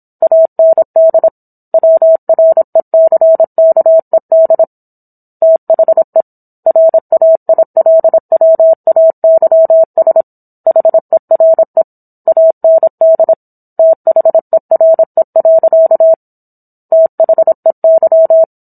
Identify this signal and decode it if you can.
Morse code, heard as a keyed tone, about 27 wpm, 650 Hz. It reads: AND WRECKED THE RAILWAYS HERE AND THERE. THEY